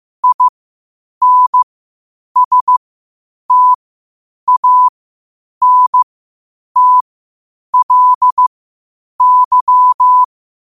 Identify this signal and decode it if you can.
Morse code, heard as a keyed tone, about 15 wpm, spaced as for 8 wpm, 1000 Hz. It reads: INSTANTLY
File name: qotdfiles / 8wpm